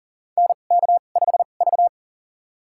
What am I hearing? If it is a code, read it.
Morse code, heard as a keyed tone, about 37 wpm, spaced as for 28 wpm, 700 Hz. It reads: NK5V
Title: USA Call Signs - 4 Characters - 2x Character Spacing - 40wpm